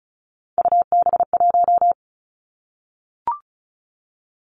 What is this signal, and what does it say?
Morse code, heard as a keyed tone, about 36 wpm, 700 Hz. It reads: UB1
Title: Three Challenging Letter-Number Combinations 35wpm